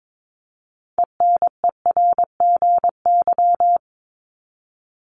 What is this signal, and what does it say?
Morse code, heard as a keyed tone, about 22 wpm, 700 Hz. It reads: ENERGY
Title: Top 751-1000 Words 22wpm